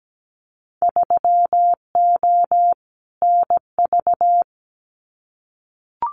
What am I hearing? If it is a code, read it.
Morse code, heard as a keyed tone, about 17 wpm, 700 Hz. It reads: 3O NV
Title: Winter Field Day Exchanges Encore 17wpm